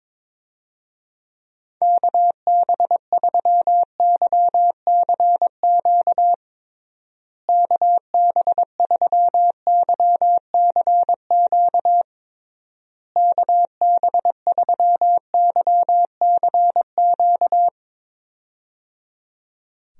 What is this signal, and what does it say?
Morse code, heard as a keyed tone, about 22 wpm, 700 Hz. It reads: KB3YCQ KB3YCQ KB3YCQ
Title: USA Call Signs - 6 Characters - Repeated 3x - 22wpm